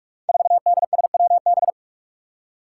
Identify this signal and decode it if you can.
Morse code, heard as a keyed tone, about 42 wpm, 700 Hz. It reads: 4DSWB
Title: Five Challenging Letter-Number Combinations 45wpm